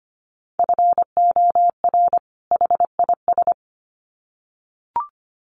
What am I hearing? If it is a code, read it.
Morse code, heard as a keyed tone, about 25 wpm, 700 Hz. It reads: FOR HIS
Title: Sets of 2 Words 25wpm